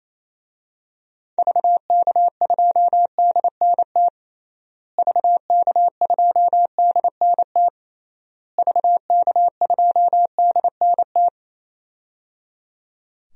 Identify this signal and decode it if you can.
Morse code, heard as a keyed tone, about 28 wpm, 700 Hz. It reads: VK2DNT VK2DNT VK2DNT
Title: DXCC-100 Call Signs - 6 Characters - Repeated 3x - 28wpm